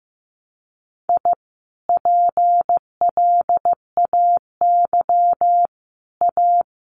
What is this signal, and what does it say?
Morse code, heard as a keyed tone, about 15 wpm, 700 Hz. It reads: I PLAY A
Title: Sentences from Top 1100 Words 15wpm